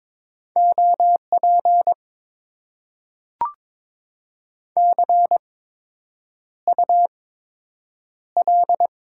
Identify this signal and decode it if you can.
Morse code, heard as a keyed tone, about 22 wpm, spaced as for 8 wpm, 700 Hz. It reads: OP CUL
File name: Top 90 QSO Elements - 8x Character Spacing - 22wpm